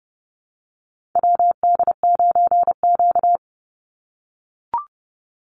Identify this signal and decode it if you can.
Morse code, heard as a keyed tone, about 30 wpm, 700 Hz. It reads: WD9Q
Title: Call Signs - 30wpm